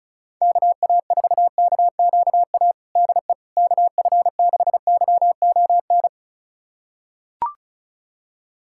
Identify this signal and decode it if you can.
Morse code, heard as a keyed tone, about 36 wpm, 700 Hz. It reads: KA4KQA DE KF6YON
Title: USA Callsign DE Callsign 35wpm